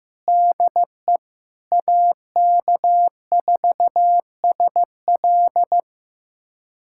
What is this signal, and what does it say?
Morse code, heard as a keyed tone, about 15 wpm, 700 Hz. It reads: DE AK4SL
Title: USA Callsign DE Callsign - 5 Characters - 15wpm